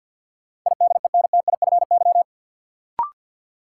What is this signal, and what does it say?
Morse code, heard as a keyed tone, about 50 wpm, 700 Hz. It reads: IDENTIFY
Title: Sentences from Top 900 Words - New Word Drill - 50wpm